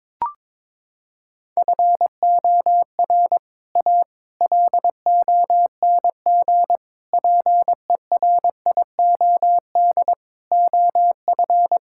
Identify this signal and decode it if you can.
Morse code, heard as a keyed tone, about 22 wpm, 700 Hz. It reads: FOR A LONG PERIOD OF
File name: practice / 22wpm / Sets of 6 Words - 22wpm